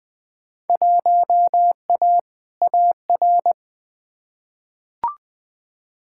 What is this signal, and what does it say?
Morse code, heard as a keyed tone, about 20 wpm, 700 Hz. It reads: 1A AR
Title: Field Day Exchanges - 20wpm